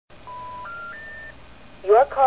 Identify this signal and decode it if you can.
Special information tone, with speech.